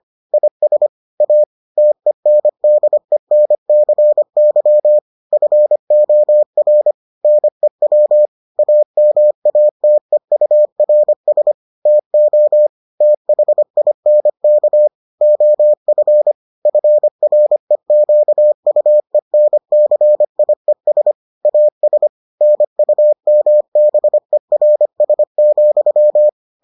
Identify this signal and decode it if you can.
Morse code, heard as a keyed tone, about 25 wpm, 600 Hz. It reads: IS A TENDENCY FOR NEW AMATEURS TO THINK OF FREQUENCIES AS NUMBERS,